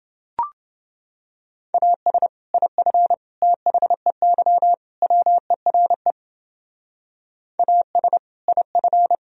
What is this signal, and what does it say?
Morse code, heard as a keyed tone, about 29 wpm, 700 Hz. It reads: AS IF THEY WERE AS IF